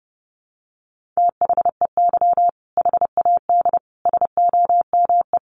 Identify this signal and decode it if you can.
Morse code, heard as a keyed tone, about 30 wpm, 700 Hz. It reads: THEY HAD SOME
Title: Sentences from Top 1500 Words 30wpm